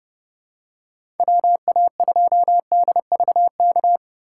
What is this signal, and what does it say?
Morse code, heard as a keyed tone, about 32 wpm, 700 Hz. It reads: WA2DVK
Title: USA Call Signs - 6 Characters - 3x Character Spacing - 30wpm